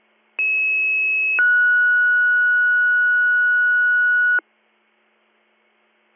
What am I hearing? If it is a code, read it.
Two-tone page, heard as a one-second tone, then a three-second tone, about 2600 then 1460 Hz. It is heard from 0.4 to 4.4 s.